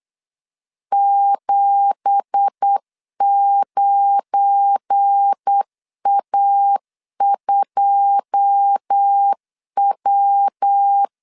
Morse code, heard as a keyed tone, about 8 wpm, 800 Hz. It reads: 79A2W
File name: CW / DET / 1 9 1500C S B 8wpm 5624